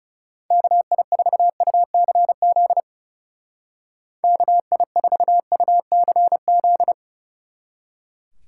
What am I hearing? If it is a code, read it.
Morse code, heard as a keyed tone, about 37 wpm, 700 Hz. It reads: KI4UCZ KI4UCZ
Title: Speed Racing - USA 2x3 Call Signs 30wpm